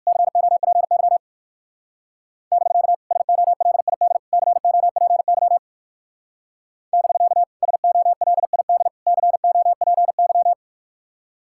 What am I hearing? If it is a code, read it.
Morse code, heard as a keyed tone, about 58 wpm, 700 Hz. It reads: COPY <BK> SOLID COPY <BK> SOLID COPY